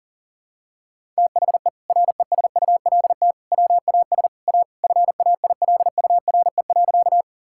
Morse code, heard as a keyed tone, about 40 wpm, 700 Hz. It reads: THE RESULT WAS A FAILURE.
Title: Sentences from Top 1000 Words - Review Part 6 - 40wpm